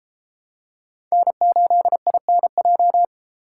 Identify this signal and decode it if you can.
Morse code, heard as a keyed tone, about 30 wpm, 700 Hz. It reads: N8INJ